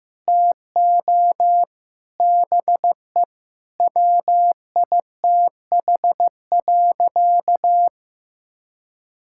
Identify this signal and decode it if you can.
Morse code, heard as a keyed tone, about 15 wpm, 700 Hz. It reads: TO BE WITH.
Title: Sentences from Top 1000 Words - Review Part 5 - 15wpm